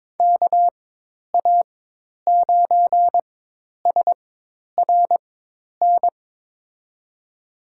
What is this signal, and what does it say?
Morse code, heard as a keyed tone, about 22 wpm, spaced as for 10 wpm, 700 Hz. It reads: KA9SRN
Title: USA Call Signs - 6 Characters - 4x Character Spacing - 22wpm